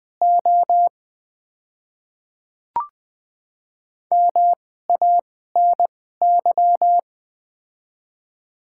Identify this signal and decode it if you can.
Morse code, heard as a keyed tone, about 20 wpm, spaced as for 15 wpm, 700 Hz. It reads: O MANY
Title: 2x Character Spacing - Top 100 Words - 20wpm